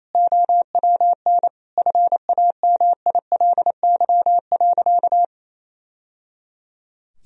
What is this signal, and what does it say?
Morse code, heard as a keyed tone, about 28 wpm, 700 Hz. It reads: OWN FAMILY.
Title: Sentences from Top 1000 Words - Review Part 1 - 28wpm